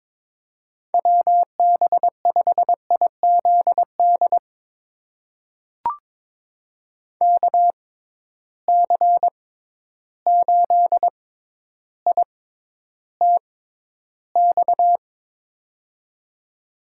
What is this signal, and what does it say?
Morse code, heard as a keyed tone, about 22 wpm, spaced as for 11 wpm, 700 Hz. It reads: WB5IZD KC8ITX
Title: USA Call Signs - 6x Character Spacing - 22wpm